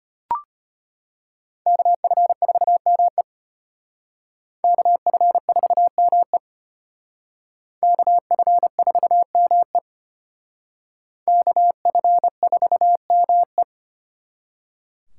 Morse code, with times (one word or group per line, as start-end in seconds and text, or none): 1.7-3.2 KF4ME
4.6-6.4 KF4ME
7.8-9.8 KF4ME
11.3-13.6 KF4ME